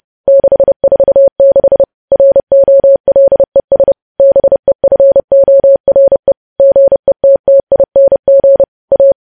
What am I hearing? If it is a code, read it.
Morse code, heard as a keyed tone, about 30 wpm, 550 Hz. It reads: 646 ROLES BEFORE GETTING A
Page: news headlines in morse code at 30 wpm